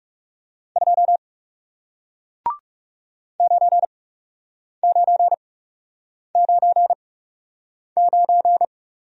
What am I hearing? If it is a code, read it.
Morse code, heard as a keyed tone, about 40 wpm, spaced as for 8 wpm, 700 Hz. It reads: 2 9999